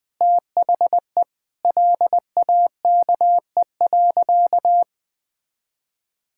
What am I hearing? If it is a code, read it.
Morse code, heard as a keyed tone, about 20 wpm, 700 Hz. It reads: THE LAKE.
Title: Sentences with 6-Word Phrases - Spoken Beginning - 20wpm